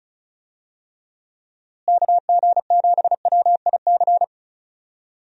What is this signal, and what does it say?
Morse code, heard as a keyed tone, about 36 wpm, 700 Hz. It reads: KG7WIC
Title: USA Call Signs - 6 Characters - Repeated 2x - 35wpm